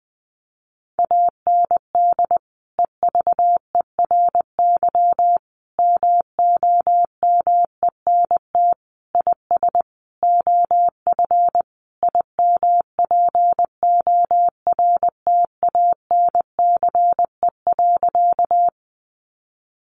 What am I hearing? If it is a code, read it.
Morse code, heard as a keyed tone, about 20 wpm, 700 Hz. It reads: AND EVERY MOMENT IS OF IMPORTANCE.
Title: Sentences from Top 1200 Words 20wpm